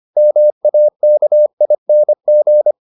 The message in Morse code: MAKING

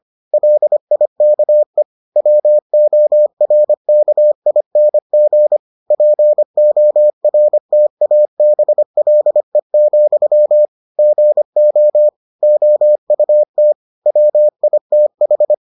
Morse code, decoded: LIKE WORKING PORTABLE, GO OUT WITH

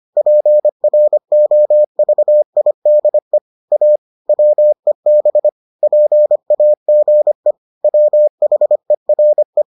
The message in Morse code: PROVIDE A WEB PAGE WHERE

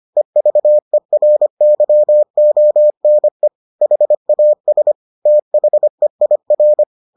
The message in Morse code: EVERYONE HAS THEIR

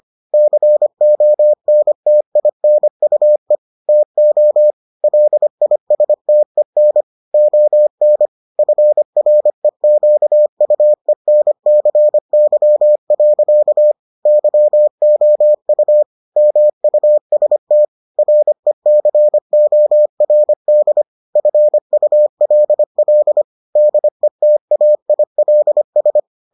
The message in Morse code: CONTINUE TO LISTEN ON FREQUENCY. YOU MUST RECORD FULL DETAILS

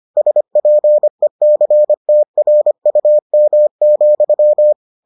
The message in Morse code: SPECTRUM,

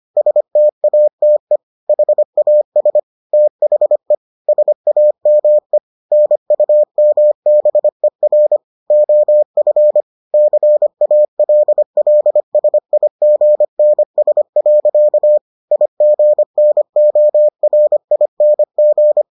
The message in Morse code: STATE HAS THE SAME NUMBER OF CALLSIGNS. IGNORING

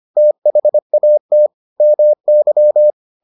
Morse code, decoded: THAT MY

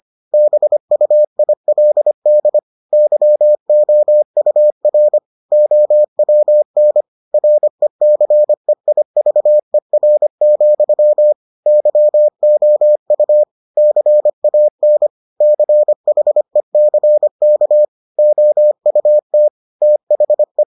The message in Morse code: BUILD YOUR OWN RECEIVER, YOU CAN CHECK OUT THE

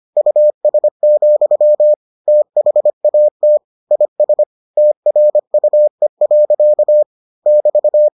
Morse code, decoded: US, THAT IS TRUE. <BT>

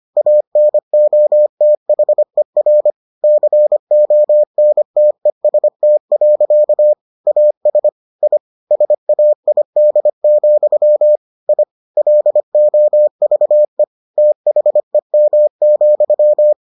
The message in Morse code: ANOTHER CONTEST. AS I SAID, I LOVE THEM,

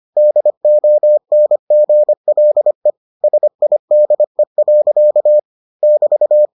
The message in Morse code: DONGLE SIDE. <BT>

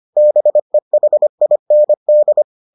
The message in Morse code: BEHIND

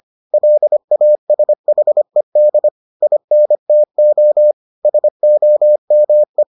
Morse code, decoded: LASHED INTO SOME

